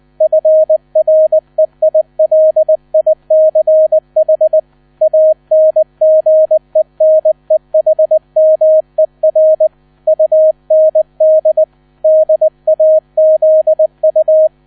Morse code, heard as a keyed tone, about 20 wpm, 630 Hz. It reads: FREILICH ANGENEHMER UND DAZU
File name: 000 Vorwort 20wpm